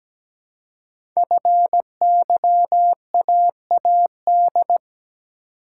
Morse code, heard as a keyed tone, about 17 wpm, 700 Hz. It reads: FYAAD
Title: Code Groups - 5 Letters - 17wpm